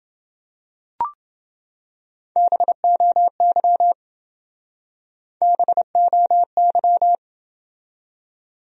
Morse code, heard as a keyed tone, about 28 wpm, 700 Hz. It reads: BOY BOY